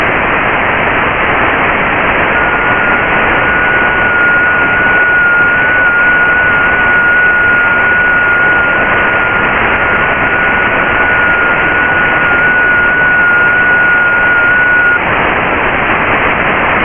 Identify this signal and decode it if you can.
FT8.
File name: Radio / SDR Audio / audio z 19-48-00 11-09-2022